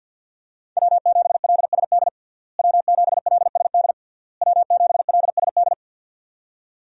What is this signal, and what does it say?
Morse code, heard as a keyed tone, about 53 wpm, 700 Hz. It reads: W7LSD W7LSD W7LSD
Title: USA Call Signs - 5 Characters - Repeated 3x - 50wpm